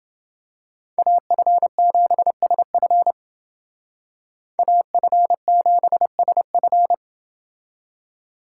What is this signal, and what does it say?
Morse code, heard as a keyed tone, about 28 wpm, 700 Hz. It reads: AF7SF AF7SF